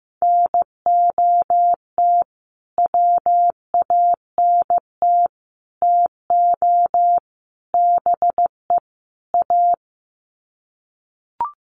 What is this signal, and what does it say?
Morse code, heard as a keyed tone, about 15 wpm, 700 Hz. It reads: NOT WANT TO BE A
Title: Sets of 5 Words Encore 15wpm